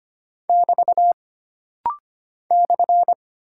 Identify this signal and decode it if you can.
Morse code, heard as a keyed tone, about 25 wpm, 700 Hz. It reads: <BT> /
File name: Single Punctuation-Prosign - Rapid-Fire - 25wpm